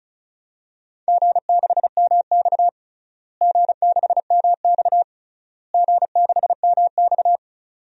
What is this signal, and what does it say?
Morse code, heard as a keyed tone, about 36 wpm, 700 Hz. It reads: G6MX G6MX G6MX
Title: DXCC-100 Call Signs - 4 Characters - Repeated 3x - 35wpm